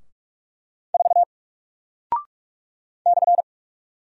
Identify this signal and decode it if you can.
Morse code, heard as a keyed tone, about 46 wpm, 700 Hz. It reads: 4 /
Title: Single Character (Letter:Number:Punctuation:Prosign) Alt 45wpm